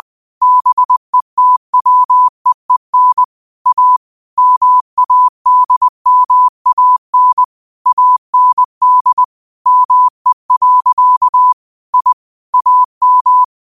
Morse code, heard as a keyed tone, about 20 wpm, 1000 Hz. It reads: BETWEEN A MADMAN AND ME. I AM